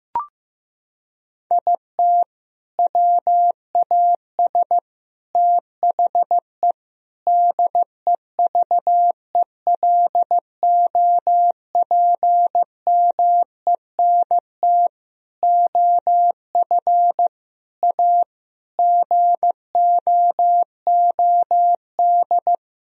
Morse code, heard as a keyed tone, about 15 wpm, 700 Hz. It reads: IT WAS THE DEVELOPMENT OF A GOOD
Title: Sentences from Top 1000 Words - Review Part 4 - 15wpm